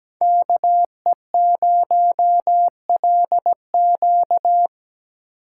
Morse code, heard as a keyed tone, about 17 wpm, 700 Hz. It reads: KE0LQ
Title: USA Call Signs - 4x Character Spacing - 17wpm